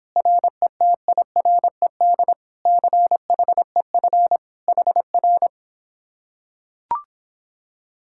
Morse code, heard as a keyed tone, about 26 wpm, 700 Hz. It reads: RETIRED CHEF HR